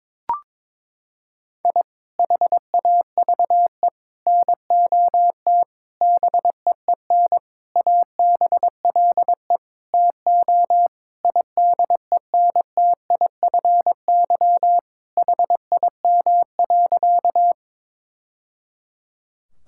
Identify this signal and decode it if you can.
Morse code, heard as a keyed tone, about 22 wpm, 700 Hz. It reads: I HAVE NOT BEEN ABLE TO IDENTIFY HIM.
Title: Sentences from Top 1000 Words - Review Part 6 - 22wpm